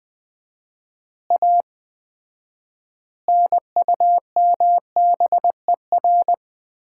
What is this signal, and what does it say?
Morse code, heard as a keyed tone, about 20 wpm, 700 Hz. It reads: A NUMBER
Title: 4x Word-Spacing - Sets of 5 Words - 20wpm